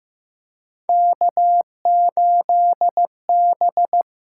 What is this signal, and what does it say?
Morse code, heard as a keyed tone, about 15 wpm, 700 Hz. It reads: K8B
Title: USA Call Signs - 5 Characters - 6x Character Spacing - 15wpm